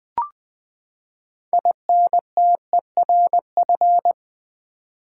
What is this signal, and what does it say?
Morse code, heard as a keyed tone, about 20 wpm, 700 Hz. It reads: INTERF
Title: Build A Word - 6 Letters At A Time - 12-Letter Words - 20wpm